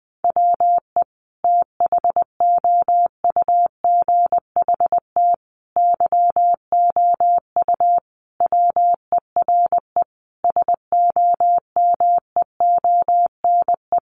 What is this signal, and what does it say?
Morse code, heard as a keyed tone, about 20 wpm, 700 Hz. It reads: WE THOUGHT YOU WERE SOMEONE
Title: Sentences from Top 800 Words 20wpm